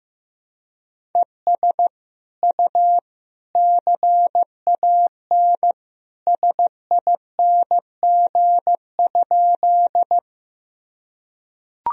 Morse code, heard as a keyed tone, about 15 wpm, 700 Hz. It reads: ES U CAN SING?
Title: Sentences from Top 400 Words and Abbreviations 15wpm